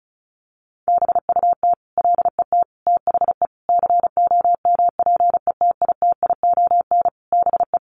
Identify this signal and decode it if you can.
Morse code, heard as a keyed tone, about 34 wpm, 700 Hz. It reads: BUT LET THE COMPETITION BE
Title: Sentences from Top 1300 Words 35wpm